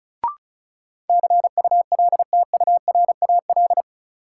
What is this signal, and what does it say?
Morse code, heard as a keyed tone, about 36 wpm, 700 Hz. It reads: CULTURAL
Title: Sentences from Top 1500 Words - New Word Drill Alt - 35wpm